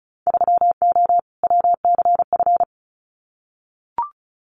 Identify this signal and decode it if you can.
Morse code, heard as a keyed tone, about 36 wpm, 700 Hz. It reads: 3O WCF
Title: Winter Field Day Exchanges 35wpm